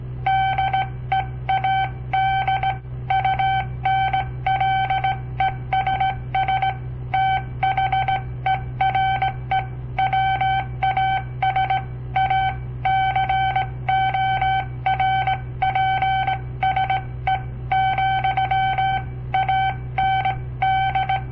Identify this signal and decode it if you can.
Morse code, heard as a keyed tone, about 17 wpm, 770 Hz. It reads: DEAD UNLESS THERE WAS A CORPSE, AND
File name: papertapes / local / Code Practice-01